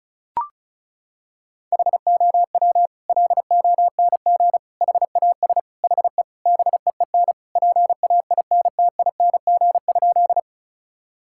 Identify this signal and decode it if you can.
Morse code, heard as a keyed tone, about 36 wpm, 700 Hz. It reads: HOW LONG HAS HE BEEN PAINTING?
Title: Sentences from Top 1000 Words - Review Part 6 - 35wpm